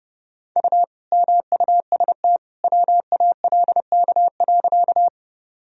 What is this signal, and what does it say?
Morse code, heard as a keyed tone, about 30 wpm, 700 Hz. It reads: U MUST WALK.
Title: Sentences from Top 300 Words and Abbreviations 30wpm